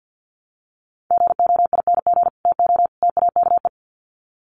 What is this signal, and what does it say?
Morse code, heard as a keyed tone, about 48 wpm, 700 Hz. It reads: GOING TO TAKE
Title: Sets of 3 Words 50wpm